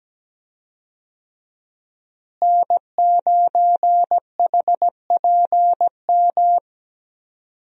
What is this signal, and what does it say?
Morse code, heard as a keyed tone, about 17 wpm, 700 Hz. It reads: N9HPM
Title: USA Call Signs - 5 Characters - Repeated 3x - 17wpm